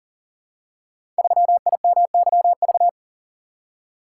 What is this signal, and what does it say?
Morse code, heard as a keyed tone, about 42 wpm, 700 Hz. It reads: 3IMYV
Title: Code Groups - 5 Letter:Numbers - 40wpm